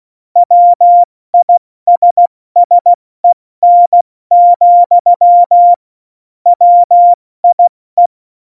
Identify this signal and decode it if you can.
Morse code, heard as a keyed tone, about 16 wpm, 700 Hz. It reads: WISSEN, WIE
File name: Morsekurs / Witze / Witz 25. Juni 2019, 70 F80